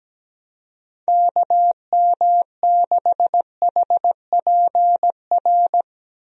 Morse code, heard as a keyed tone, about 17 wpm, 700 Hz. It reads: KM6HPR
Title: USA Call Signs - 6 Characters - Repeated 3x - 17wpm